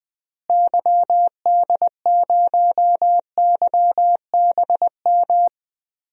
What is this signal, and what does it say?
Morse code, heard as a keyed tone, about 20 wpm, 700 Hz. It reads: YD0YBM
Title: DXCC-100 Call Signs - 6 Characters - Repeated 3x - 20wpm